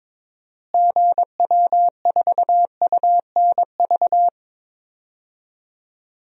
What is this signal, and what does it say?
Morse code, heard as a keyed tone, about 22 wpm, 700 Hz. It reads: GW4UNV
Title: DXCC-100 Call Signs - 6 Characters - Repeated 3x - 22wpm